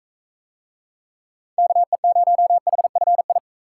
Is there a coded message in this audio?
Yes: Morse code from 1.6 to 3.4 s, heard as a keyed tone, about 44 wpm, 700 Hz. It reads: KE0HFI